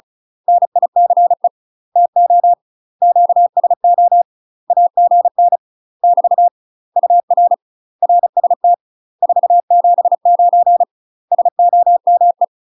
Morse code, heard as a keyed tone, about 36 wpm, 700 Hz. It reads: NICE TO QSO AGN <BT> UR RST 479 SOME
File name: UK MASTER QSOs TRAINING / 35WPM / UK MASTER QSOs 0023